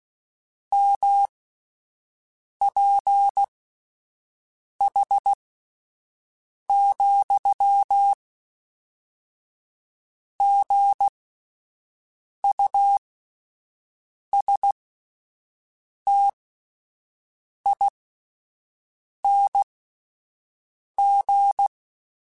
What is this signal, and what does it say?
Morse code, heard as a keyed tone, about 16 wpm, spaced as for 5 wpm, 800 Hz. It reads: MPH,GUSTING
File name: Spaced Characters / Spaced Characters-mp3s / Weather4 21 sp @10